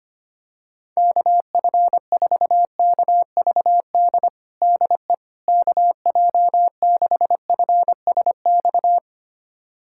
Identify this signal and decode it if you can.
Morse code, heard as a keyed tone, about 25 wpm, 700 Hz. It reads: KF4KVD DE KJ6FSX
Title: USA Callsign DE Callsign - 6 Characters - 25wpm